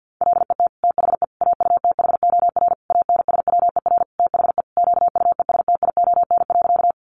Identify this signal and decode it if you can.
Morse code, heard as a keyed tone, about 50 wpm, 700 Hz. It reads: LET THE AUTHOR ANSWER THE QUESTION.
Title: Sentences from Top 1200 Words 50wpm